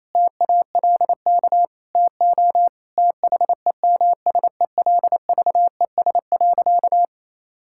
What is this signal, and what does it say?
Morse code, heard as a keyed tone, about 28 wpm, 700 Hz. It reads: TALK TO THEMSELVES.
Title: Sentences from Top 1000 Words - Review Part 5 - 28wpm